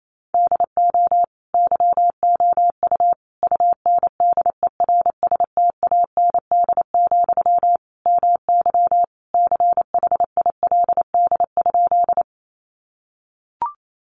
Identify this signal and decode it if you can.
Morse code, heard as a keyed tone, about 28 wpm, 700 Hz. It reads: DO YOU UNDERSTAND, MY CHILD?
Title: Sentences from Top 800 Words 28wpm